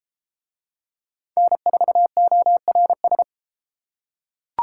Morse code, heard as a keyed tone, about 29 wpm, 700 Hz. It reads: N4ORS